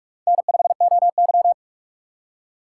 Morse code, heard as a keyed tone, about 47 wpm, 700 Hz. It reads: N5OY
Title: USA Call Signs - 3x Character Spacing - 45wpm